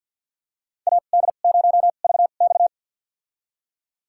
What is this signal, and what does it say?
Morse code, heard as a keyed tone, about 52 wpm, spaced as for 36 wpm, 690 Hz. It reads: AD0VX
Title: USA Call Signs - 2x Character Spacing - 50wpm